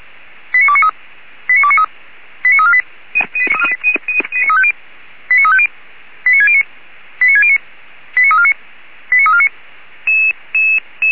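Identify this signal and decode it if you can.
Five-tone selective call (ZVEI), heard as quick runs of five tones, tones of about 70 ms. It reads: 89293 89293 89348 89350 89609 89609 89349 89349
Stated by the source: ZVEI five-tone selective call received by SDR, address 89248.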